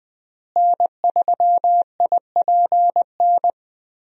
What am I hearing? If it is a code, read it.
Morse code, heard as a keyed tone, about 20 wpm, 700 Hz. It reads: N3IPN
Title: USA Call Signs - 5 Characters - Repeated 3x - 20wpm